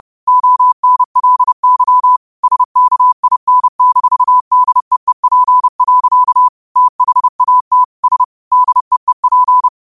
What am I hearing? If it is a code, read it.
Morse code, heard as a keyed tone, about 30 wpm, 1000 Hz. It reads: ONLY SKIN-DEEP. THAT S DEEP
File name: qotdfiles / 30wpm